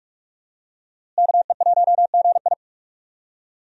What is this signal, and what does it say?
Morse code, heard as a keyed tone, about 42 wpm, 700 Hz. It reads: KE1GI
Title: USA Call Signs - 5 Characters - Repeated 2x - 45wpm